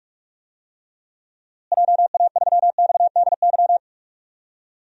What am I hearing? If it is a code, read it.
Morse code, heard as a keyed tone, about 41 wpm, 700 Hz. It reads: JA3XDY
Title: DXCC-100 Call Signs - 6 Characters - Repeated 2x - 45wpm